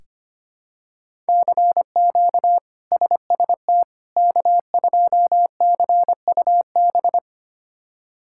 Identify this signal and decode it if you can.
Morse code, heard as a keyed tone, about 25 wpm, 700 Hz. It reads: CQ SST K2CUB
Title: K1USN SST QSO Practice - 25wpm